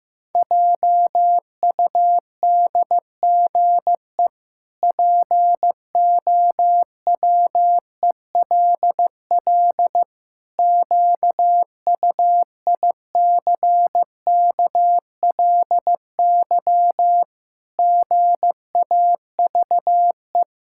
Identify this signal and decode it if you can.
Morse code, heard as a keyed tone, about 15 wpm, 700 Hz. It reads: JUDGE POWELL QUICKLY GAVE
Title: Pangram Challenge v2 - 15wpm